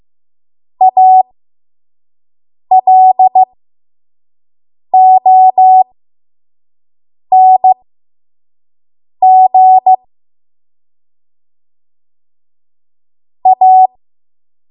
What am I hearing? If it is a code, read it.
Morse code, heard as a keyed tone, about 15 wpm, spaced as for 5 wpm, 750 Hz. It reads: ALONG A
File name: Morse / 05WPM